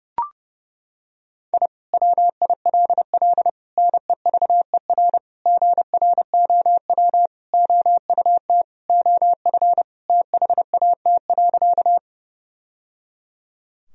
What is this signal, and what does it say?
Morse code, heard as a keyed tone, about 30 wpm, 700 Hz. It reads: I WILL NEVER GROW OUT OF THAT.
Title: Sentences from Top 1000 Words - Review Part 1 - 30wpm